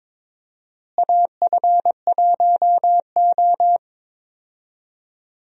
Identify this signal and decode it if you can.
Morse code, heard as a keyed tone, about 21 wpm, 700 Hz. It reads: AF1O
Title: USA Call Signs - 4 Characters - Repeated 2x - 22wpm